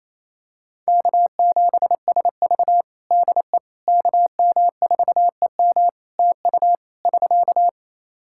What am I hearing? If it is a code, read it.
Morse code, heard as a keyed tone, about 28 wpm, 700 Hz. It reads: K7SV DE KM4EM TU <SK>
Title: Practice QSO - Exchange 4 - Ending - 28wpm